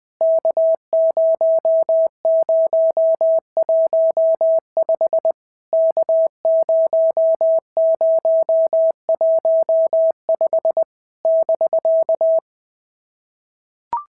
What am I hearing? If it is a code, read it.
Morse code, heard as a keyed tone, about 20 wpm, 650 Hz. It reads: K0015 K0015 <BK>
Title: POTA QSO Practice - 20wpm